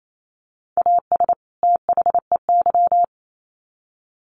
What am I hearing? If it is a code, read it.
Morse code, heard as a keyed tone, about 28 wpm, 700 Hz. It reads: AS THEY
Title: Sets of 2 Words 28wpm